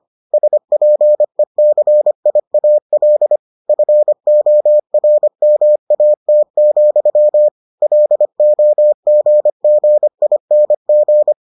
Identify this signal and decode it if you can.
Morse code, heard as a keyed tone, about 25 wpm, 600 Hz. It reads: SPECIAL FORMAT, LOGGING